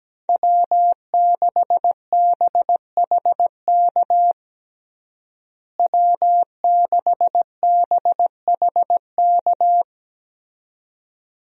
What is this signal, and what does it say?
Morse code, heard as a keyed tone, about 17 wpm, 700 Hz. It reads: W6BHK W6BHK
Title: USA Call Signs - 5 Characters - Repeated 2x - 17wpm